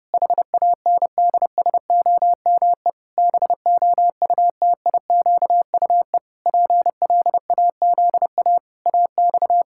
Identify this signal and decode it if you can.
Morse code, heard as a keyed tone, about 32 wpm, 700 Hz. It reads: HANDSOME BOUTIQUE PLAZA AX
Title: Pangram Challenge v2 - 30wpm